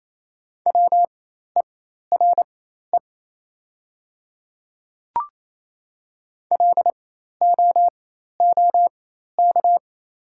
Morse code, heard as a keyed tone, about 28 wpm, spaced as for 13 wpm, 700 Hz. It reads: WERE LOOK
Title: 4x Character Spacing - Top 100 Words - 28wpm